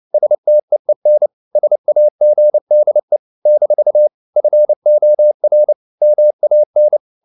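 Morse code, heard as a keyed tone, about 29 wpm, 600 Hz. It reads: STEEN SAGDE - FOR MAN
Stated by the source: Fast (QRQ) Morse at 29 wpm, from CW med Gnister.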